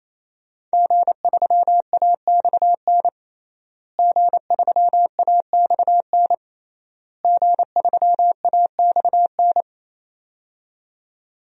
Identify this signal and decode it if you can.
Morse code, heard as a keyed tone, about 28 wpm, 700 Hz. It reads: G3AXN G3AXN G3AXN
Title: DXCC-100 Call Signs - 5 Characters - Repeated 3x - 28wpm